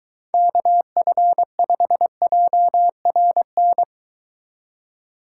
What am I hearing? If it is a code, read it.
Morse code, heard as a keyed tone, about 23 wpm, 700 Hz. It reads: KF5JRN